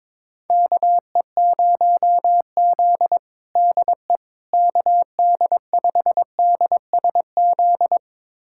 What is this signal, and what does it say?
Morse code, heard as a keyed tone, about 22 wpm, 700 Hz. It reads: KE0Z DE KD5DSZ
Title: USA Callsign DE Callsign 22wpm